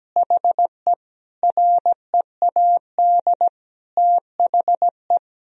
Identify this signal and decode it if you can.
Morse code, heard as a keyed tone, about 17 wpm, 700 Hz. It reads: HE READ THE
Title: Sentences from Top 1000 Words - Review Part 3 - 17wpm